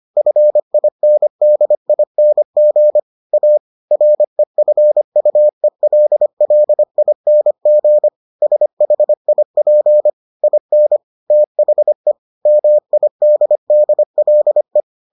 Morse code, heard as a keyed tone, about 25 wpm, 600 Hz. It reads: FINDING A REFUELLING SHIP IN THE MIDDLE